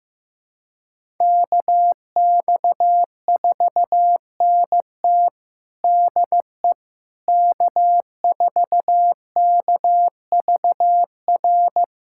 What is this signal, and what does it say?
Morse code, heard as a keyed tone, about 15 wpm, 700 Hz. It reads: KX4NT DE K4KVR
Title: USA Callsign DE Callsign - 5 Characters - 15wpm